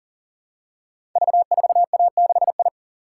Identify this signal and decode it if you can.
Morse code, heard as a keyed tone, about 40 wpm, 690 Hz. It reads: V4A6I
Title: Code Groups - 5 Letter:Numbers - 40wpm